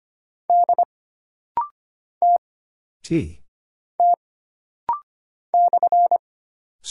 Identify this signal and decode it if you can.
Morse code, heard as a keyed tone, about 25 wpm, 700 Hz. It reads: D T T /
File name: Single Character (Letter:Number:Punctuation:Prosign) - Rapid-Fire - 25wpm